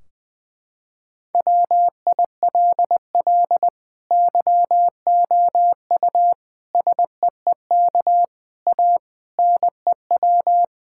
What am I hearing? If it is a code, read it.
Morse code, heard as a keyed tone, about 21 wpm, 700 Hz. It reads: WILL YOU SEEK A NEW
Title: Sentences from Top 1600 Words 20wpm